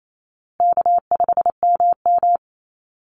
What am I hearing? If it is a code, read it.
Morse code, heard as a keyed tone, about 28 wpm, 700 Hz. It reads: K5MM
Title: Easy Call Signs Repeated 3x - 28wpm